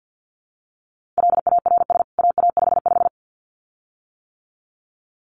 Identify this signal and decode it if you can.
Morse code, heard as a keyed tone, about 50 wpm, 700 Hz. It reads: LARS AA55
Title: CWT Exchange 50wpm